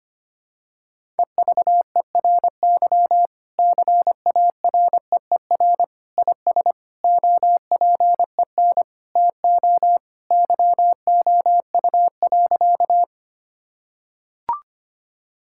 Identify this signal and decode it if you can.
Morse code, heard as a keyed tone, about 25 wpm, 700 Hz. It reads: EVERY CAREER IS OPEN TO YOU.
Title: Sentences from Top 1000 Words - Review Part 6 - 25wpm